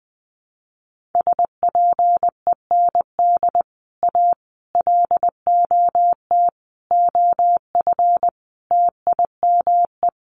Morse code, heard as a keyed tone, about 19 wpm, 700 Hz. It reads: SPEND A LOT OF TIME
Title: Sets of 5 Words 20wpm